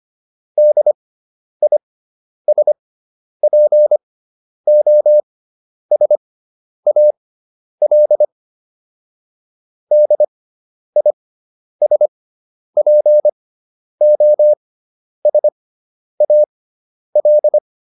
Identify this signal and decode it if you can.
Morse code, heard as a keyed tone, about 25 wpm, spaced as for 10 wpm, 600 Hz. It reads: DISPOSAL DISPOSAL